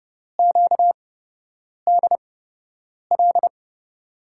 Morse code, heard as a keyed tone, about 30 wpm, spaced as for 8 wpm, 700 Hz. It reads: QDL
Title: USA Call Signs - 6 Characters - 8x Character Spacing - 30wpm